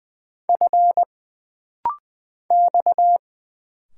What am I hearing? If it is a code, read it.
Morse code, heard as a keyed tone, about 20 wpm, 700 Hz. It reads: F X
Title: Single Letters - Rapid-Fire 20wpm